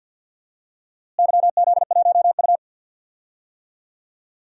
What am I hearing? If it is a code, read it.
Morse code, heard as a keyed tone, about 52 wpm, 700 Hz. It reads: YZ1U
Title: DXCC-100 Call Signs - 4 Characters - Repeated 3x - 50wpm